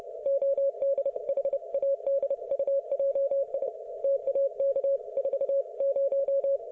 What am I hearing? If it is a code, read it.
Morse code, heard as a keyed tone, about 28 wpm, 550 Hz. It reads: ODHADUJI TAK 40
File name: Archiv / CUC181 / ok1fhi cul181